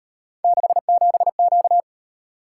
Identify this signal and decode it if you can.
Morse code, heard as a keyed tone, about 38 wpm, 700 Hz. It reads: 67Q